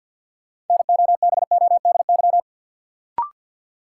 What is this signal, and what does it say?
Morse code, heard as a keyed tone, about 52 wpm, 700 Hz. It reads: NOBODY
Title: Sentences from Top 1600 Words - New Word Drill Alt - 50wpm